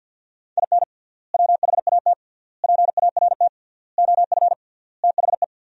Morse code, heard as a keyed tone, about 50 wpm, 700 Hz. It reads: IN WHAT PART OF THE
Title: 3x Word-Spacing - Sentences from Top 200 Words - 50wpm